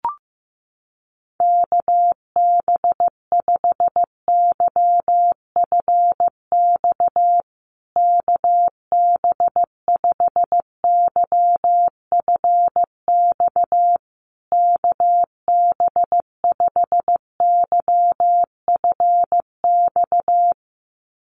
Morse code, heard as a keyed tone, about 15 wpm, 700 Hz. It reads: KB5YFX KB5YFX KB5YFX
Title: USA Call Signs Repeated 3x 15wpm